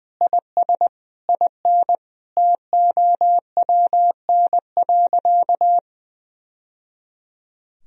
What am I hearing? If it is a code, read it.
Morse code, heard as a keyed tone, about 21 wpm, 700 Hz. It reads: IS IN TOWN.
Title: Sentences from Top 1000 Words - Review Part 2 - 20wpm